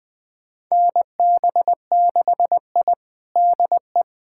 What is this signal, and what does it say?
Morse code, heard as a keyed tone, about 21 wpm, 700 Hz. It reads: NB6I DE
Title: USA Callsign DE Callsign - 4 Characters - 20wpm